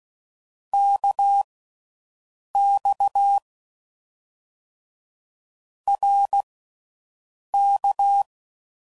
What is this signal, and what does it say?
Morse code, heard as a keyed tone, about 16 wpm, spaced as for 6 wpm, 800 Hz. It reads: KX RK